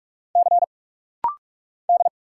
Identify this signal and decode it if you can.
Morse code, heard as a keyed tone, about 46 wpm, 700 Hz. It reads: C D
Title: Single Letters - Rapid-Fire 45wpm